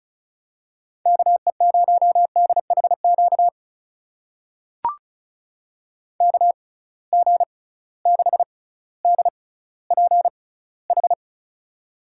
Morse code, heard as a keyed tone, about 36 wpm, spaced as for 18 wpm, 700 Hz. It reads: KE0DHQ KG6DPH
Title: USA Call Signs - 6 Characters - 6x Character Spacing - 35wpm